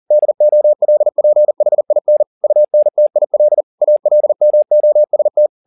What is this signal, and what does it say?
Morse code, heard as a keyed tone, about 40 wpm, 600 Hz. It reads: DOLPHIN UNTIL ALMOST
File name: the old man and the sea 2 40wpm